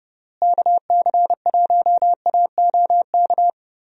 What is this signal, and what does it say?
Morse code, heard as a keyed tone, about 32 wpm, 700 Hz. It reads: KC1AOK
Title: USA Call Signs - 6 Characters - Repeated 2x - 30wpm